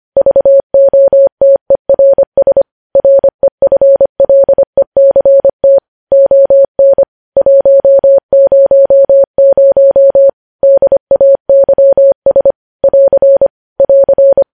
Morse code, heard as a keyed tone, about 25 wpm, 550 Hz. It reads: VOTERS REFLECT ON 100 DAYS <AR> <AR>
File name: podcast / News headlines-1x-25x25